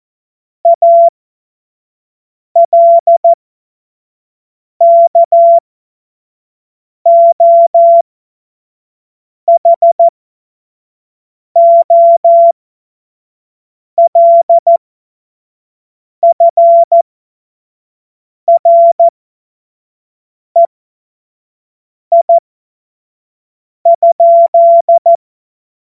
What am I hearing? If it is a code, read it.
Morse code, heard as a keyed tone, about 14 wpm, spaced as for 5 wpm, 680 Hz. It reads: ALKOHOLFREI?
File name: Morsekurs / Witze / Witz 22. Juni 2021, 25 F70